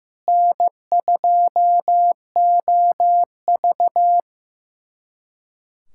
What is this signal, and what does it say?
Morse code, heard as a keyed tone, about 15 wpm, 700 Hz. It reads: N2OV
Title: USA Callsign DE Callsign - 4 Characters - 15wpm